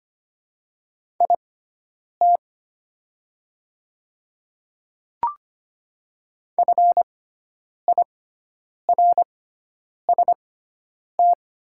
Morse code, heard as a keyed tone, about 25 wpm, spaced as for 9 wpm, 700 Hz. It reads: IT FIRST
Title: 6x Character Spacing - Top 100 Words - 25wpm